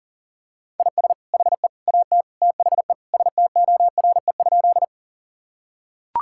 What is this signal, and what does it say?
Morse code, heard as a keyed tone, about 40 wpm, 700 Hz. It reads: IS HE AT THE STORE?
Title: Sentences from Top 1000 Words - Review Part 5 - 40wpm